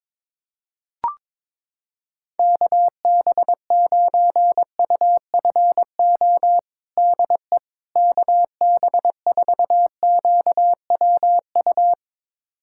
Morse code, heard as a keyed tone, about 22 wpm, 700 Hz. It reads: KB9UFO DE KB4QWU
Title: USA Callsign DE Callsign - 6 Characters - 22wpm